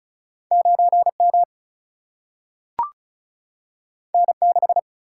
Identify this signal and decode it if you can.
Morse code, heard as a keyed tone, about 36 wpm, 700 Hz. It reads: 9M N6
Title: Single-Letter Single-Number 35wpm